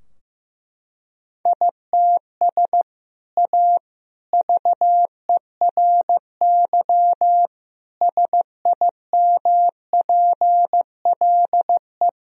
Morse code, heard as a keyed tone, about 15 wpm, 700 Hz. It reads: ITS A VERY SIMPLE
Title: Sentences from Top 1000 Words - Review Part 3 - 15wpm